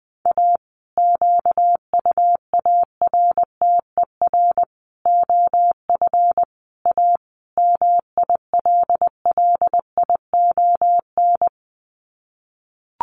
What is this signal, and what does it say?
Morse code, heard as a keyed tone, about 20 wpm, 700 Hz. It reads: A QUARTER OF A MILLION
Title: Sets of 5 Words Encore2 20wpm